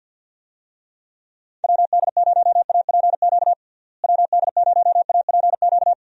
Morse code, heard as a keyed tone, about 49 wpm, 700 Hz. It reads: WD0APQ WD0APQ
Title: USA Call Signs - 6 Characters - Repeated 3x - 50wpm